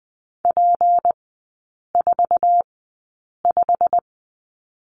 Morse code, heard as a keyed tone, about 20 wpm, spaced as for 8 wpm, 700 Hz. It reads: P45
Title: 152 Lesson 25e - Letter p - Three Characters 20wpm